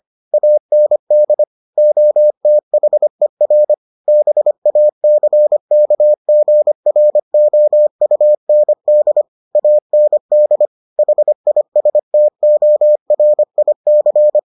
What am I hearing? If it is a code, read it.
Morse code, heard as a keyed tone, about 25 wpm, 600 Hz. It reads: AND OTHER BACKGROUND AND HISTORIC